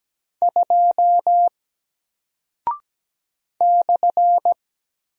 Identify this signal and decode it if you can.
Morse code, heard as a keyed tone, about 17 wpm, 700 Hz. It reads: 2 /
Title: Single Character (Letter:Number:Punctuation:Prosign) Alt 17wpm